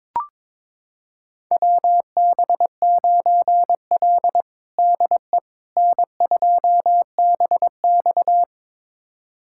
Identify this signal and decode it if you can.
Morse code, heard as a keyed tone, about 22 wpm, 700 Hz. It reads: WB9L DE N2BX
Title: USA Callsign DE Callsign - 4 Characters - 22wpm